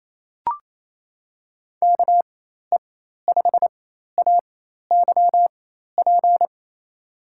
Morse code, heard as a keyed tone, about 28 wpm, spaced as for 13 wpm, 700 Hz. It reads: KE5AYP
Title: USA Call Signs - 6 Characters - 4x Character Spacing - 28wpm